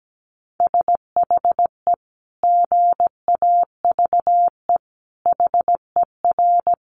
Morse code, heard as a keyed tone, about 17 wpm, 700 Hz. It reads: SHE GAVE HER
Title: Sentences from Top 1500 Words 17wpm